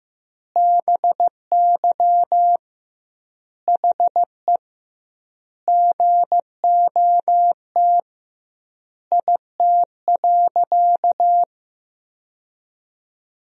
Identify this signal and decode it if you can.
Morse code, heard as a keyed tone, about 15 wpm, 700 Hz. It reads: BY HE GOT IT.
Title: 2x Word-Spacing - Sentences from Top 300 Words - 15wpm